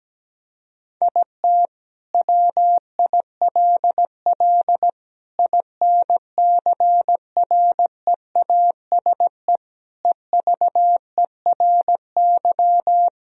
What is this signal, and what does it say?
Morse code, heard as a keyed tone, about 17 wpm, 700 Hz. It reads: IT WILL INCREASE EVERY
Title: Sentences from Top 1000 Words - Review Part 4 - 17wpm